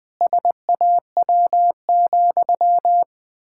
Morse code, heard as a keyed tone, about 20 wpm, 700 Hz. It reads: SAW,
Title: Pangram Challenge v2 - 20wpm